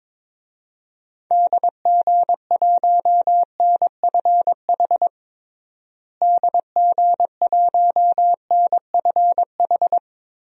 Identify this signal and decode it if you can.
Morse code, heard as a keyed tone, about 22 wpm, 700 Hz. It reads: DG1NFH DG1NFH
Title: DXCC-100 Call Signs - 6 Characters - Repeated 3x - 22wpm